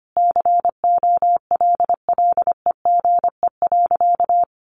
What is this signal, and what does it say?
Morse code, heard as a keyed tone, about 25 wpm, 700 Hz. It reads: COLLEGE.
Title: Sentences from Top 1300 Words 25wpm